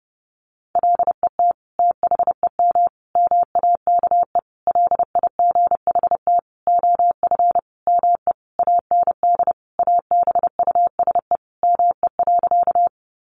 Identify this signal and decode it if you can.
Morse code, heard as a keyed tone, about 30 wpm, 700 Hz. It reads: LET THEM MAKE LIGHT OF ME AND ABUSE ME.
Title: Sentences from Top 1300 Words 30wpm